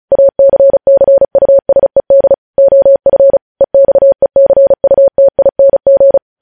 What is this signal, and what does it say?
Morse code, heard as a keyed tone, about 36 wpm, 550 Hz. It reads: ACCUSED OF EXECUTING
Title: News Headlines at 35